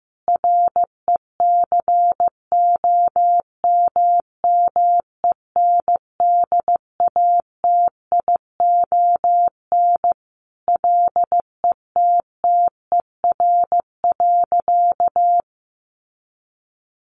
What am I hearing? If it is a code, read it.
Morse code, heard as a keyed tone, about 15 wpm, 700 Hz. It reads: RECOMMENDATION LETTER.
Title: Sentences from Top 1400 Words 15wpm